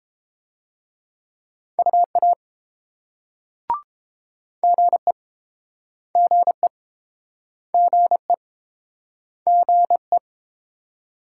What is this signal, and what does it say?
Morse code, heard as a keyed tone, about 32 wpm, 690 Hz. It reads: UA GE GE GE GE